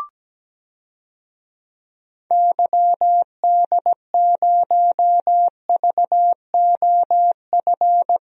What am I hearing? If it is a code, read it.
Morse code, heard as a keyed tone, about 17 wpm, 700 Hz. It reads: YD0VOF
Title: DXCC-100 Call Signs - 6 Characters - 17wpm